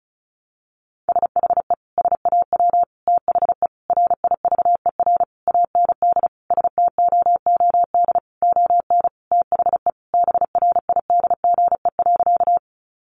Morse code, heard as a keyed tone, about 34 wpm, 700 Hz. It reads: SHE SAW THE RIVER AND STOOD ON THE BRIDGE.
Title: Sentences from Top 1200 Words 35wpm